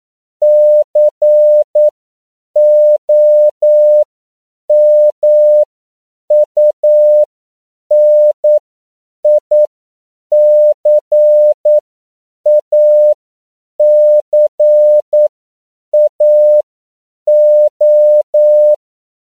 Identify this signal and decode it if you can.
Morse code, heard as a keyed tone, about 9 wpm, spaced as for 7 wpm, 600 Hz. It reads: COMUNICACAO
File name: CW / 10PPM